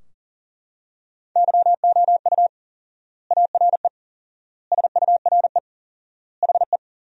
Morse code, heard as a keyed tone, about 37 wpm, 700 Hz. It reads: YOU ARE SURE HE
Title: 4x Word-Spacing - Sentences from Top 300 Words - 40wpm